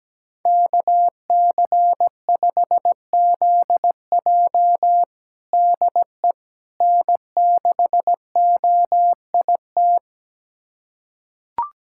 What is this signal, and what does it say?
Morse code, heard as a keyed tone, about 17 wpm, 700 Hz. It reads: KC5ZJ DE N6OIT
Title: USA Callsign DE Callsign - 5 Characters - 17wpm